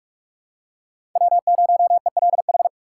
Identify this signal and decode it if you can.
Morse code, heard as a keyed tone, about 46 wpm, 700 Hz. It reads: W0ELH
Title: USA Call Signs - 5 Characters - 4x Character Spacing - 45wpm